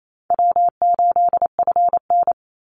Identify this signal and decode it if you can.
Morse code, heard as a keyed tone, about 28 wpm, 700 Hz. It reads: W8FN
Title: Call Signs Repeated 2x - 28wpm